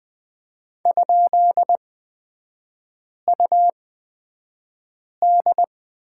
Morse code, heard as a keyed tone, about 20 wpm, spaced as for 5 wpm, 700 Hz. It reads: ?UD
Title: 224 Lesson 36e - BT - Three Characters 20wpm